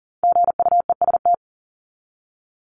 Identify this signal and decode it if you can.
Morse code, heard as a keyed tone, about 40 wpm, 700 Hz. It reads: GUEST
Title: Top 1501-1750 Words 40wpm